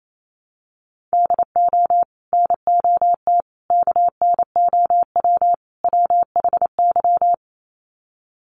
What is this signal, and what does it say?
Morse code, heard as a keyed tone, about 28 wpm, 700 Hz. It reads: DO NOT KNOW WHY
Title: Sets of 4 Words 28wpm